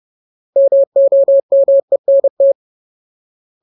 Morse code, heard as a keyed tone, about 30 wpm, 550 Hz. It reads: MOMENT
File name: Ultimate Word List / 30WPM Ultimate